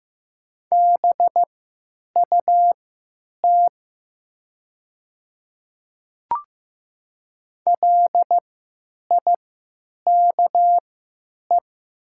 Morse code, heard as a keyed tone, about 15 wpm, spaced as for 9 wpm, 700 Hz. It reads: BUT LIKE
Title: 3x Character Spacing - Top 100 Words - 15wpm